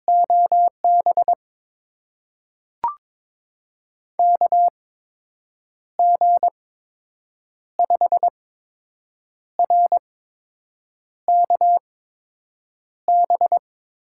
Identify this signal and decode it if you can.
Morse code, heard as a keyed tone, about 22 wpm, spaced as for 7 wpm, 700 Hz. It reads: OB KG5RKB
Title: USA Call Signs - 6 Characters - 8x Character Spacing - 22wpm